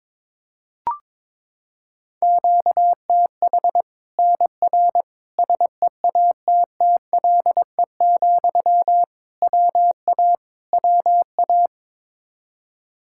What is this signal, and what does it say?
Morse code, heard as a keyed tone, about 22 wpm, 700 Hz. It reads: QTH NR SEATTLE, WA WA
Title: Practice QSO - Exchange 1 - Beginning - 22wpm